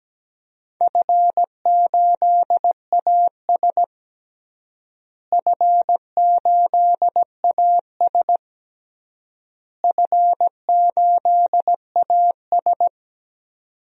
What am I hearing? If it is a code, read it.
Morse code, heard as a keyed tone, about 17 wpm, 700 Hz. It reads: F8AS F8AS F8AS
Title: DXCC-100 Call Signs - 4 Characters - Repeated 3x - 17wpm